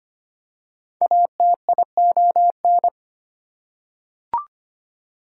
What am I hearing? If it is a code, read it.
Morse code, heard as a keyed tone, about 25 wpm, 700 Hz. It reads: ATION
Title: Prefixes and Suffixes 25wpm